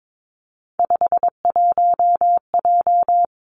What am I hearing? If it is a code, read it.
Morse code, heard as a keyed tone, about 22 wpm, 700 Hz. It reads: 51J
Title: Three Challenging Letter-Number Combinations 22wpm